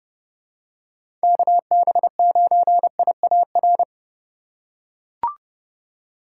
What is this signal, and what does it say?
Morse code, heard as a keyed tone, about 30 wpm, 700 Hz. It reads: KB9IAR